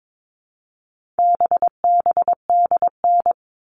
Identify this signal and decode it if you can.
Morse code, heard as a keyed tone, about 22 wpm, 700 Hz. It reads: BBDN
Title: All Cut Numbers - 4-Digit 22wpm